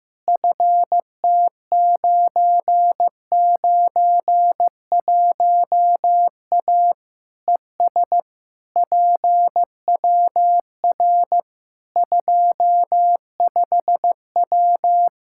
Morse code, heard as a keyed tone, about 15 wpm, 700 Hz. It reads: FT991A ES PWR 25W
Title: Practice QSO - Exchange 2 - Info - 15wpm